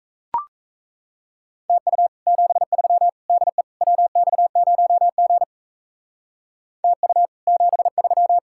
Morse code, heard as a keyed tone, about 36 wpm, 700 Hz. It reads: TU 73 DE WX0G TU 73